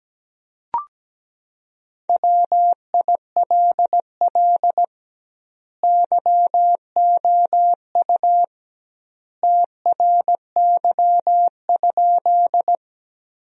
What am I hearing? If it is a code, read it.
Morse code, heard as a keyed tone, about 17 wpm, 700 Hz. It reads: WILL YOU TRY?
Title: 2x Word-Spacing - Sentences from Top 200 Words - 17wpm